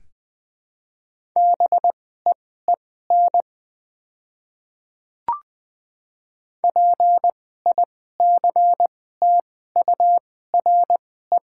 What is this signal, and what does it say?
Morse code, heard as a keyed tone, about 20 wpm, spaced as for 14 wpm, 700 Hz. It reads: BEEN PICTURE
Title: 2x Character Spacing - Top 500 Words - 20wpm